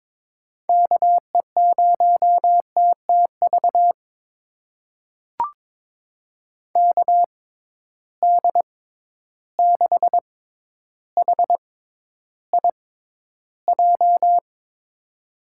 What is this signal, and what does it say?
Morse code, heard as a keyed tone, about 22 wpm, spaced as for 11 wpm, 700 Hz. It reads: KE0TTV KD6HIJ
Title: USA Call Signs - 6 Characters - 6x Character Spacing - 22wpm